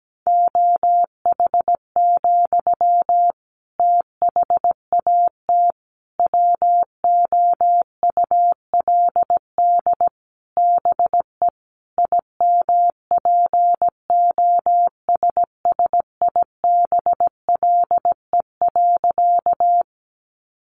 Morse code, heard as a keyed tone, about 17 wpm, 700 Hz. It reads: OH, THAT WOULD BE IMPOSSIBLE.
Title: Sentences from Top 1100 Words 17wpm